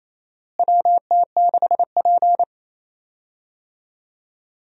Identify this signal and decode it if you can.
Morse code, heard as a keyed tone, about 27 wpm, 700 Hz. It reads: WT6P
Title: USA Call Signs - 4 Characters - Repeated 2x - 28wpm